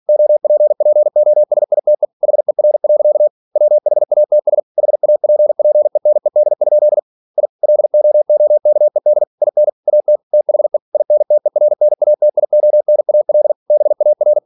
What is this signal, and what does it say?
Morse code, heard as a keyed tone, about 46 wpm, 600 Hz. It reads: OPPOSITE HER. WHATS HAPPENED? I LOOKED IN AT THE INTERNATIONAL BAR
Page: Never 47wpm